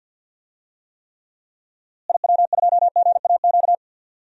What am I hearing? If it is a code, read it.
Morse code, heard as a keyed tone, about 48 wpm, 700 Hz. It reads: IW2GAQ
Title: DXCC-100 Call Signs - 6 Characters - Repeated 3x - 50wpm